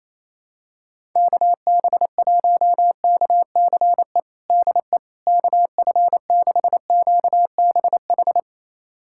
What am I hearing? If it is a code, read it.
Morse code, heard as a keyed tone, about 28 wpm, 700 Hz. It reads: KB1KCE DE KF6QBH
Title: USA Callsign DE Callsign 28wpm